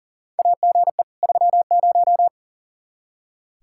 Morse code, heard as a keyed tone, about 40 wpm, 700 Hz. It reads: AGE 30
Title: Practice QSO - Exchange 3 - More Info - 40wpm